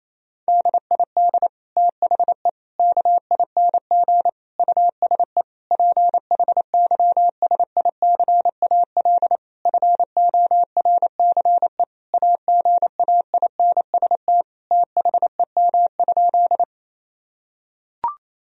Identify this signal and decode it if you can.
Morse code, heard as a keyed tone, about 28 wpm, 700 Hz. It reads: DID THE KING USE PHYSICAL FORCE AGAINST THEM?
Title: Sentences from Top 1000 Words - Review Part 4 - 28wpm